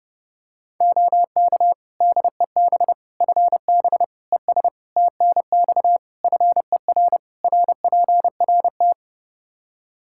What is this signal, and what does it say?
Morse code, heard as a keyed tone, about 30 wpm, 700 Hz. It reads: OK DEB FB ES TNX FER RPRT
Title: Practice QSO - Exchange 2 - Info - 30wpm